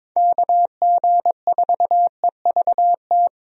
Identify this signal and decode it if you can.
Morse code, heard as a keyed tone, about 22 wpm, 700 Hz. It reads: KG4EVT
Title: USA Call Signs - 6 Characters - 22wpm